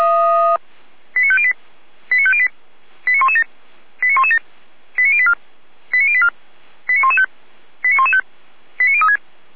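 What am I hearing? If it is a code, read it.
Five-tone selective call (ZVEI), heard as quick runs of five tones, tones of about 70 ms. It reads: 89508 89508 89117 89117 89084 89084 89115 89115 89036
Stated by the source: ZVEI five-tone selective call 89092 received via SDR.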